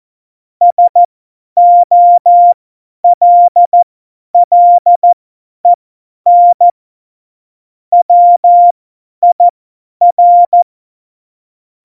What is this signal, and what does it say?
Morse code, heard as a keyed tone, about 14 wpm, spaced as for 10 wpm, 700 Hz. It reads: SOLLEN WIR
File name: Morsekurs / Witze / Witz 13. November 2018, 50 F70